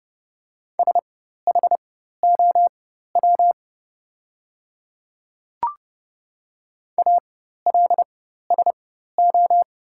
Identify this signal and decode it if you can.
Morse code, heard as a keyed tone, about 30 wpm, spaced as for 14 wpm, 690 Hz. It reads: SHOW ALSO
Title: 4x Character Spacing - Top 200 Words - 30wpm